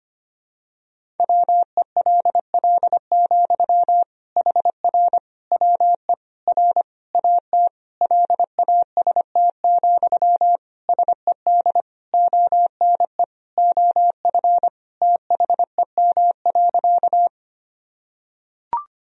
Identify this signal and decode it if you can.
Morse code, heard as a keyed tone, about 25 wpm, 700 Hz. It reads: WELL, HR WE R AT LAST, SED ONE OF THEM.
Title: Sentences from Top 300 Words and Abbreviations 25wpm